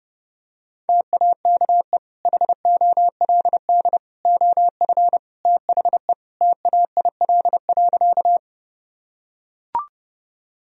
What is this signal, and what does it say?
Morse code, heard as a keyed tone, about 30 wpm, 700 Hz. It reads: TAKE HOLD OF THE TAIL.
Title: Sentences from Top 1000 Words - Review Part 3 - 30wpm